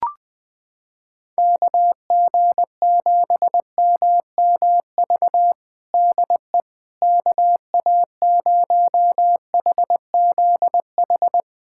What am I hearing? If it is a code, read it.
Morse code, heard as a keyed tone, about 20 wpm, 700 Hz. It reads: KG7MMV DE KA0HZH
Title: USA Callsign DE Callsign 20wpm